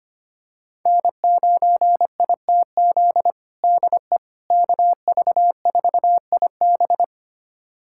Morse code, heard as a keyed tone, about 25 wpm, 700 Hz. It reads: N9ITZ DE KV4IB
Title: USA Callsign DE Callsign - 5 Characters - 25wpm